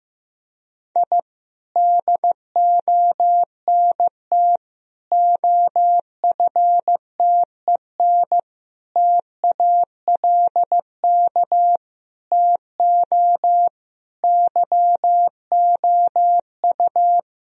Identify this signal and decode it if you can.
Morse code, heard as a keyed tone, about 15 wpm, 700 Hz. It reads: I DONT OFTEN TALK TO YOU
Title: Sentences from Top 1000 Words - Review Part 1 - 15wpm